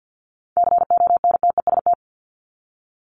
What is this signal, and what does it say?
Morse code, heard as a keyed tone, about 48 wpm, 700 Hz. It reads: CONTEST
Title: Top 1501-1750 Words 50wpm